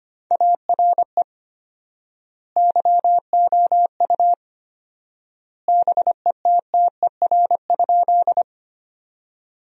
Morse code, heard as a keyed tone, about 25 wpm, 700 Hz. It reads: ARE YOU BETTER?
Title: 4x Word-Spacing - Sentences from Top 400 Words - 25wpm